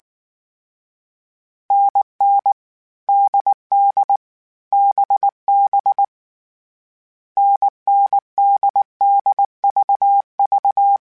Morse code, heard as a keyed tone, about 19 wpm, 800 Hz. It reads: NN DD BB NNDDVV
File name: Difficult Combos / Difficult Combos-mp3s / NDB NDB6 @18